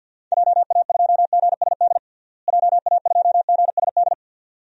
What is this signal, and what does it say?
Morse code, heard as a keyed tone, about 52 wpm, 700 Hz. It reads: JA2GSD JA2GSD
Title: DXCC-100 Call Signs - 6 Characters - Repeated 2x - 50wpm